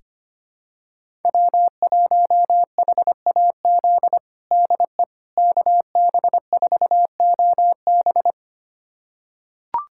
Morse code, heard as a keyed tone, about 25 wpm, 700 Hz. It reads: W1HAZ DE KB4OB
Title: USA Callsign DE Callsign - 5 Characters - 25wpm